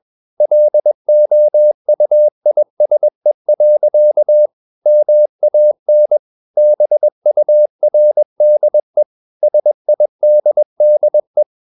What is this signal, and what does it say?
Morse code, heard as a keyed tone, about 21 wpm, 600 Hz. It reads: LOUISE. MAN BURDE SIDDE
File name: Ferie QRS 21wpm